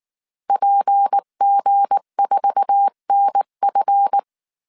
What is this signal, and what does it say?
Morse code, heard as a keyed tone, about 19 wpm, 800 Hz. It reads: PG4NF
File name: CW / DET / 1 9 1500C S B 18wpm 2551